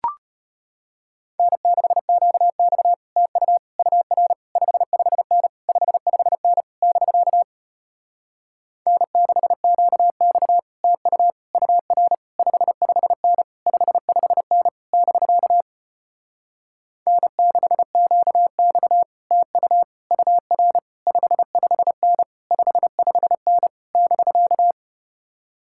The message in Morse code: N6QX TU UR 55N 55N <BK> N6QX TU UR 55N 55N <BK> N6QX TU UR 55N 55N <BK>

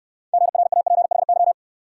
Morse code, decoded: GRAPHY

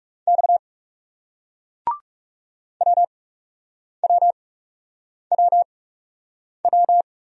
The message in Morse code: X WWWW

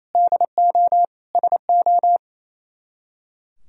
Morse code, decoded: DO SO

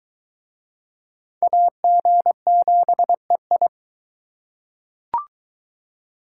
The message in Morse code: AG7EI

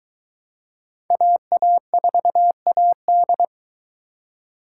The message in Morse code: AA4AD

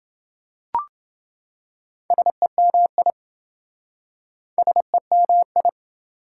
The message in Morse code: SEMI SEMI

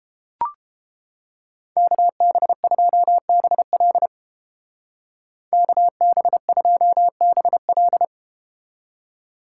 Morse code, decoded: KB2BL KB2BL

